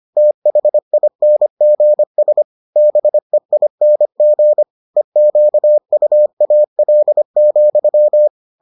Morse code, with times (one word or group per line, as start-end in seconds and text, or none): 0.2-2.4 THINGS
2.8-4.6 BEING
5.0-8.3 EQUAL,